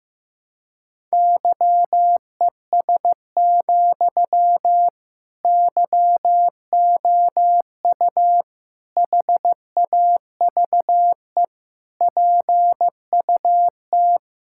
Morse code, heard as a keyed tone, about 15 wpm, 700 Hz. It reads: YES, YOU HAVE PUT
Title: Sentences from Top 1600 Words 15wpm